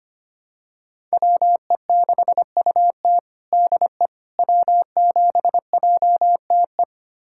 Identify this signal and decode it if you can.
Morse code, heard as a keyed tone, about 25 wpm, 700 Hz. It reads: WE6UT DE W7JTE
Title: USA Callsign DE Callsign - 5 Characters - 25wpm